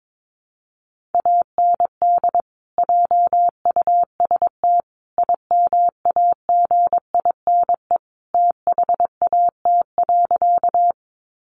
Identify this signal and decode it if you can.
Morse code, heard as a keyed tone, about 22 wpm, 700 Hz. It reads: AND JUST IMAGINE THAT.
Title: Sentences from Top 1400 Words 22wpm